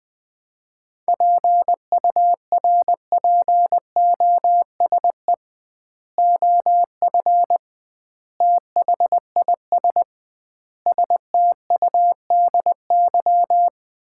Morse code, heard as a keyed tone, about 20 wpm, 700 Hz. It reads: PURPOSE OF THIS STUDY
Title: 2x Word-Spacing - Sets of 5 Words - 20wpm